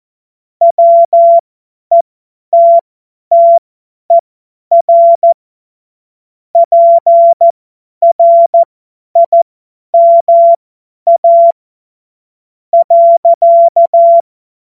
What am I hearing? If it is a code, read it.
Morse code, heard as a keyed tone, about 14 wpm, spaced as for 9 wpm, 680 Hz. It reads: WETTER PRIMA.
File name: Morsekurs / Witze / Witz 21. März 2023 50 F70